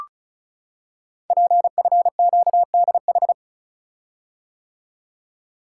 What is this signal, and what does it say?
Morse code, heard as a keyed tone, about 36 wpm, 700 Hz. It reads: PFQDH
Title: Code Groups - 5 Letters - 35wpm